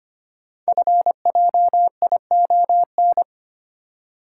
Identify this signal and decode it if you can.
Morse code, heard as a keyed tone, about 25 wpm, 700 Hz. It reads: FJION
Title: Code Groups - 5 Letters - 25wpm